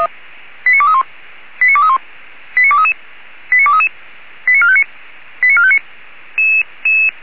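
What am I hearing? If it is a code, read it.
Five-tone selective call (ZVEI), heard as quick runs of five tones, tones of about 70 ms. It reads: 89231 89231 89233 89233 89459 89459
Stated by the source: ZVEI five-tone selective call 89240 received by SDR.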